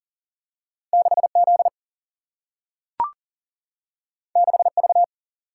Morse code, heard as a keyed tone, about 39 wpm, 700 Hz. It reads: 6Z 6V
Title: Single-Letter Single-Number 40wpm